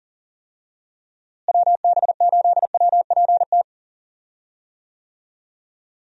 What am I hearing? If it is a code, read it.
Morse code, heard as a keyed tone, about 40 wpm, 700 Hz. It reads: WB8WPT
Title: USA Call Signs - 6 Characters - Repeated 3x - 40wpm